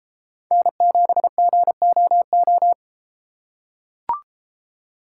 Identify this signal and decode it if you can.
Morse code, heard as a keyed tone, about 33 wpm, 700 Hz. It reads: N7GOO